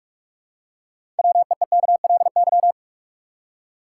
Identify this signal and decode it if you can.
Morse code, heard as a keyed tone, about 47 wpm, 700 Hz. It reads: WEEKLY
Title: Six Letter Words - 45wpm